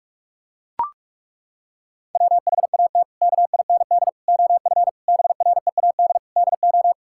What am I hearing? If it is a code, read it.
Morse code, heard as a keyed tone, about 41 wpm, 700 Hz. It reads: WHAT KIND OF BREAD DO
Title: Sentences from Top 1600 Words 45wpm